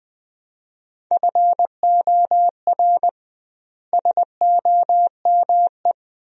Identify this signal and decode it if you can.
Morse code, heard as a keyed tone, about 20 wpm, 700 Hz. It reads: FOR SOME
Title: 2x Word-Spacing - Sentences from Top 200 Words - 20wpm